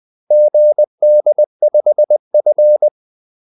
Morse code, heard as a keyed tone, about 20 wpm, 600 Hz. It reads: GD5F
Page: Kaldesignaler 20 wpm